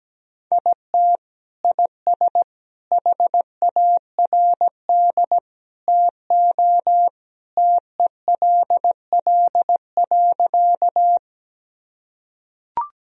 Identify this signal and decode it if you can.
Morse code, heard as a keyed tone, about 17 wpm, 700 Hz. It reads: IT IS HARD TO TELL.
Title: Sentences from Top 1000 Words - Review Part 1 - 17wpm